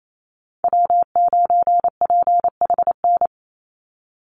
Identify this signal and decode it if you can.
Morse code, heard as a keyed tone, about 28 wpm, 700 Hz. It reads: W9PHN
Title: USA Call Signs 28wpm